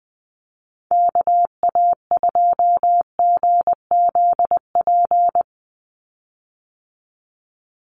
Morse code, heard as a keyed tone, about 20 wpm, 700 Hz. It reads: KA2GZP
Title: USA Call Signs 20wpm